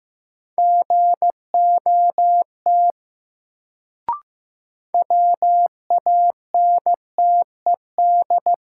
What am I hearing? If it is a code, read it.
Morse code, heard as a keyed tone, about 15 wpm, 700 Hz. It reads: GOT WANTED
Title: Sets of 2 Words - New Word Drill - 15wpm